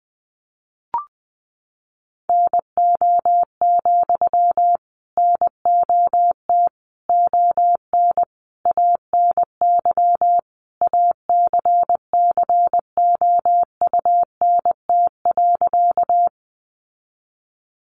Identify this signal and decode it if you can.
Morse code, heard as a keyed tone, about 20 wpm, 700 Hz. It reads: NO, NOT ON ANY ACCOUNT.
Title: Sentences from Top 900 Words 20wpm